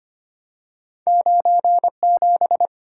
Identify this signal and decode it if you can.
Morse code, heard as a keyed tone, about 25 wpm, 700 Hz. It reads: 97
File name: Two Letter-Number Combinations - 25wpm